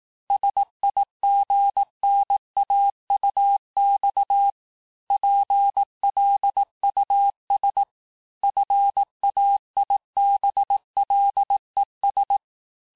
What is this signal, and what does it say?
Morse code, heard as a keyed tone, about 18 wpm, 800 Hz. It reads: SIGNAUX PLUS FAIBLES
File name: cours-du-lundi files / cours-1852-FIN-du-QSO-vitesse-17MPM